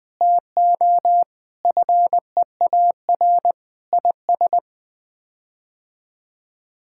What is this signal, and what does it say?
Morse code, heard as a keyed tone, about 20 wpm, 700 Hz. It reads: TO FEAR IS
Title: Sets of 8 Words - 20wpm